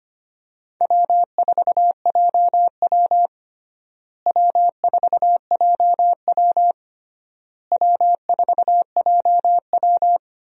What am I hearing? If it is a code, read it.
Morse code, heard as a keyed tone, about 25 wpm, 700 Hz. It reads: W4JW W4JW W4JW
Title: USA Call Signs - 4 Characters - Repeated 3x - 25wpm